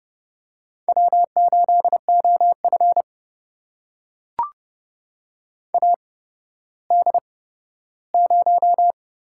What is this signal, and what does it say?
Morse code, heard as a keyed tone, about 30 wpm, spaced as for 15 wpm, 700 Hz. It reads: W8OF AD0
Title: USA Call Signs - 8x Character Spacing - 30wpm